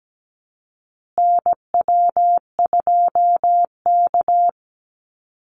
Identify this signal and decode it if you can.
Morse code, heard as a keyed tone, about 17 wpm, 700 Hz. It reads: NW2K
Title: Call Signs Repeated 2x - 17wpm